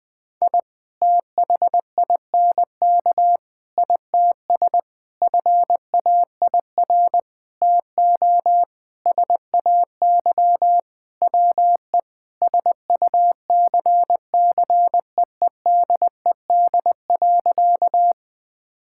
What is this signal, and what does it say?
Morse code, heard as a keyed tone, about 20 wpm, 700 Hz. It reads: I THINK ITS FAIR TO SAY WE SUCCEEDED.
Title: Sentences with 6-Word Phrases - 20wpm